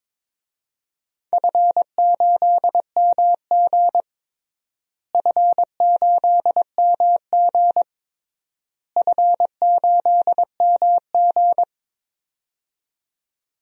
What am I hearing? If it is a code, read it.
Morse code, heard as a keyed tone, about 22 wpm, 700 Hz. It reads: F8MG F8MG F8MG
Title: DXCC-100 Call Signs - 4 Characters - Repeated 3x - 22wpm